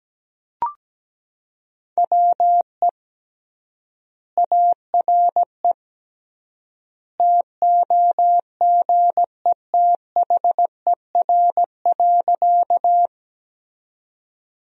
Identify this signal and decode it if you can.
Morse code, heard as a keyed tone, about 17 wpm, 700 Hz. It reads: WE ARE TOGETHER.
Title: 3x Word-Spacing - Sentences from Top 300 Words - 17wpm